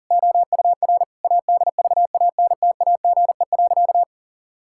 Morse code, heard as a keyed tone, about 40 wpm, 700 Hz. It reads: OUR ADVANTAGE.
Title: Sentences from Top 1000 Words - Review Part 6 - 40wpm